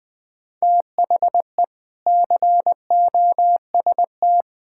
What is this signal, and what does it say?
Morse code, heard as a keyed tone, about 20 wpm, 700 Hz. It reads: THE COST
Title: Sentences with 3-Word Phrases - 20wpm